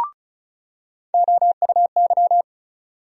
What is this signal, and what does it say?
Morse code, heard as a keyed tone, about 34 wpm, 700 Hz. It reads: OUY
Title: USA Call Sign Postfixes - 3 Letters 35wpm